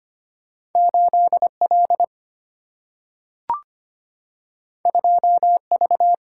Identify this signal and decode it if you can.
Morse code, heard as a keyed tone, about 25 wpm, 690 Hz. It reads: 8L 2V
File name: Two Letter-Number Combinations - 25wpm